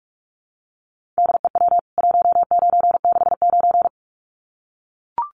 Morse code, heard as a keyed tone, about 44 wpm, 700 Hz. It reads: DEW 1969
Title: CWT Exchange 45wpm